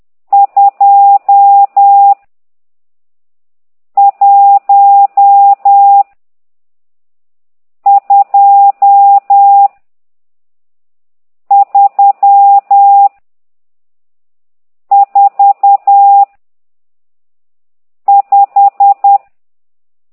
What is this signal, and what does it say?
Morse code, heard as a keyed tone, about 10 wpm, 800 Hz. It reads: 2 1 2 3 4 5
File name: ct2idw / exerc12